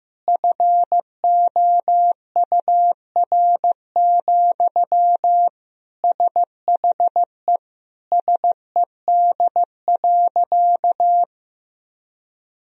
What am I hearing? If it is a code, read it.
Morse code, heard as a keyed tone, about 16 wpm, 700 Hz. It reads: FOUR, SHE SED.
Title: Sentences from Top 300 Words and Abbreviations 15wpm